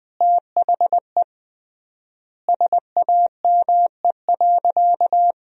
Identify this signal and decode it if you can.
Morse code, heard as a keyed tone, about 21 wpm, 700 Hz. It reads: THE SAME.
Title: 3x Word-Spacing - Sentences from Top 200 Words - 20wpm